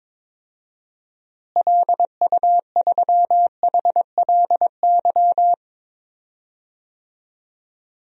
Morse code, heard as a keyed tone, about 22 wpm, 700 Hz. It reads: LU3HLY
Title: DXCC-100 Call Signs - 6 Characters - Repeated 3x - 22wpm